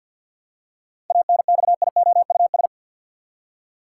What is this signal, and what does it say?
Morse code, heard as a keyed tone, about 52 wpm, 700 Hz. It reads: ANXIOUS
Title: Seven Letter Words - 50wpm